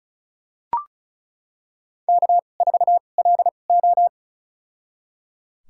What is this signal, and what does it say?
Morse code, heard as a keyed tone, about 36 wpm, spaced as for 25 wpm, 700 Hz. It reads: K4LO
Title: USA Call Signs - 2x Character Spacing - 35wpm